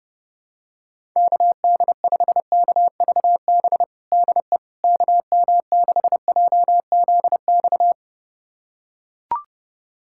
Morse code, heard as a keyed tone, about 32 wpm, 700 Hz. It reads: KD5KVB DE KM6JZX
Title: USA Callsign DE Callsign - 6 Characters - 30wpm